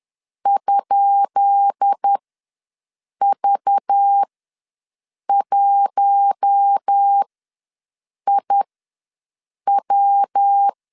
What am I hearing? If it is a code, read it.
Morse code, heard as a keyed tone, about 10 wpm, spaced as for 6 wpm, 800 Hz. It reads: ?V1IW
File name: LEZIONI / 10 SIGNS / 10C DOT ws 10wpm 4213